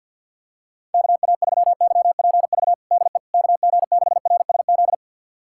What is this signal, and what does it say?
Morse code, heard as a keyed tone, about 49 wpm, 700 Hz. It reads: KA3YPV DE KG6RSZ
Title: USA Callsign DE Callsign 50wpm